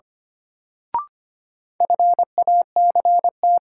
Morse code, heard as a keyed tone, about 25 wpm, 700 Hz. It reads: FACT
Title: Sentences from Top 500 Words - New Word Drill - 25wpm